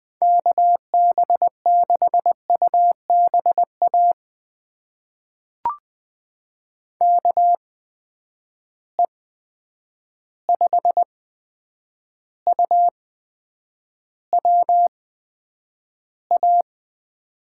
Morse code, heard as a keyed tone, about 20 wpm, spaced as for 9 wpm, 700 Hz. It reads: KB6UBA KE5UWA
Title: USA Call Signs - 6 Characters - 8x Character Spacing - 20wpm